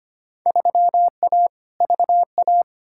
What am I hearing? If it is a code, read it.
Morse code, heard as a keyed tone, about 25 wpm, 700 Hz. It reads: 3A VA
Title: Field Day Exchanges - 25wpm